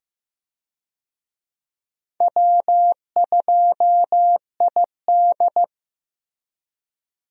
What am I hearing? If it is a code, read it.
Morse code, heard as a keyed tone, about 15 wpm, 700 Hz. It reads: W2ID
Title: USA Call Signs - 4 Characters - Repeated 3x - 15wpm